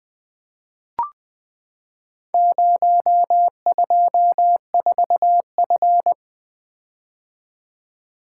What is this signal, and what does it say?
Morse code, heard as a keyed tone, about 20 wpm, 700 Hz. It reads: 024F
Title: Four Letter-Number Combinations - 20wpm